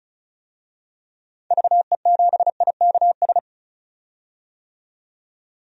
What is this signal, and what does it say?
Morse code, heard as a keyed tone, about 34 wpm, 700 Hz. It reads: VE7IKS
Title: DXCC-100 Call Signs - 6 Characters - 35wpm